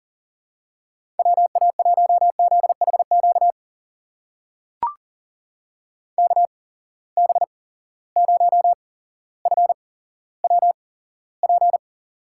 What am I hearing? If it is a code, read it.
Morse code, heard as a keyed tone, about 42 wpm, spaced as for 17 wpm, 700 Hz. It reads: WA1ZHQ KB0FWP
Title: USA Call Signs - 6 Characters - 8x Character Spacing - 40wpm